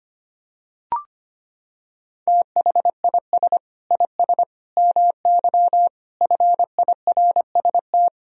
Morse code, heard as a keyed tone, about 25 wpm, 700 Hz. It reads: THIS IS MY FIRST
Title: Sentences from Top 1000 Words - Review Part 3 - 25wpm